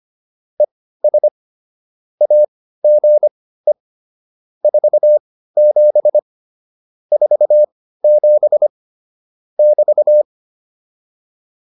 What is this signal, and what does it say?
Morse code, heard as a keyed tone, about 25 wpm, spaced as for 15 wpm, 600 Hz. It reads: ES AGE 47 47 <BT>
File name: QSO404 15